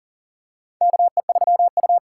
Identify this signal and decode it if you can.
Morse code, heard as a keyed tone, about 40 wpm, 700 Hz. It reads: KE3U
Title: USA Call Signs - 4 Characters - 4x Character Spacing - 40wpm